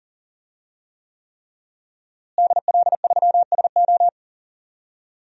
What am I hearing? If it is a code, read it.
Morse code, heard as a keyed tone, about 40 wpm, 700 Hz. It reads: DL3SO
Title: DXCC-100 Call Signs - 5 Characters - Repeated 3x - 40wpm